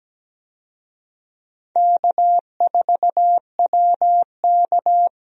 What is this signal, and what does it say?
Morse code, heard as a keyed tone, about 17 wpm, 700 Hz. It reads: K4WK
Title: USA Call Signs - 4 Characters - Repeated 2x - 17wpm